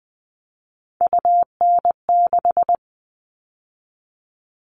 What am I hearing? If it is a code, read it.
Morse code, heard as a keyed tone, about 20 wpm, 700 Hz. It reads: UN6
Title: All Cut Numbers - 3-Digit 20wpm